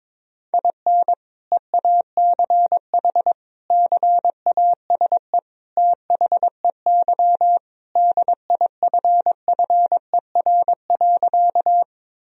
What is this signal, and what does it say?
Morse code, heard as a keyed tone, about 22 wpm, 700 Hz. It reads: IN EACH CASE THEY DIFFER.
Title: Sentences from Top 1000 Words - Review Part 1 - 22wpm